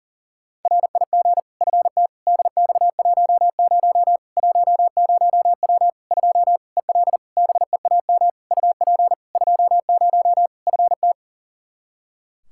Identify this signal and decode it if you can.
Morse code, heard as a keyed tone, about 40 wpm, 700 Hz. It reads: RIG FT DX10 10W 2 EL BEAM UP 20 FT
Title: Practice QSO - Exchange 2 - Info - 40wpm